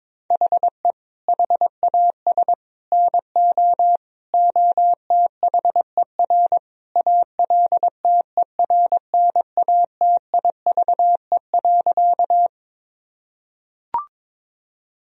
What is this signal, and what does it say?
Morse code, heard as a keyed tone, about 22 wpm, 700 Hz. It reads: HE HAS NO OTHER ALTERNATIVE.
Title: Sentences from Top 1000 Words - Review Part 4 - 22wpm